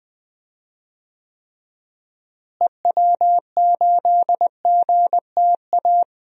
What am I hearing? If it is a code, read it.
Morse code, heard as a keyed tone, about 20 wpm, 700 Hz. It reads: EW8GTA
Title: DXCC-100 Call Signs - 6 Characters - Repeated 3x - 20wpm